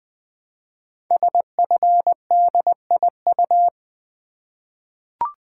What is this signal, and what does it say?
Morse code, heard as a keyed tone, about 20 wpm, 700 Hz. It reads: SFDIU
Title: Code Groups - 5 Letters - 20wpm